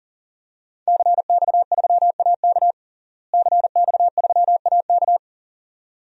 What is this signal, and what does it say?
Morse code, heard as a keyed tone, about 40 wpm, 700 Hz. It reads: CX3AK CX3AK
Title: DXCC-100 Call Signs - 5 Characters - Repeated 2x - 40wpm